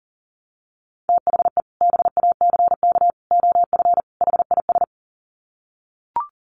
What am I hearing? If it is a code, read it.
Morse code, heard as a keyed tone, about 40 wpm, 700 Hz. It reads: THE BACK OF HIS
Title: Sets of 4 Words 40wpm